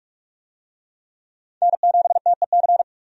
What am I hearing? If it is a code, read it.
Morse code, heard as a keyed tone, about 47 wpm, 700 Hz. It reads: N7TEC
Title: USA Call Signs - 5 Characters - 45wpm